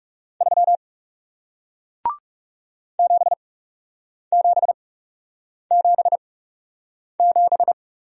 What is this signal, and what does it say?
Morse code, heard as a keyed tone, about 40 wpm, spaced as for 8 wpm, 700 Hz. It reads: 3 7777